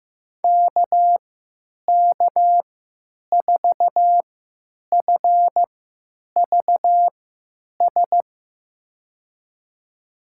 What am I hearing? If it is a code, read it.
Morse code, heard as a keyed tone, about 15 wpm, spaced as for 9 wpm, 700 Hz. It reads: KK4FVS
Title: USA Call Signs - 3x Character Spacing - 15wpm